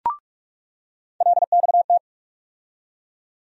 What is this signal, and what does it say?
Morse code, heard as a keyed tone, about 46 wpm, 700 Hz. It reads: LXT
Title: USA Call Sign Postfixes - 3 Letters 45wpm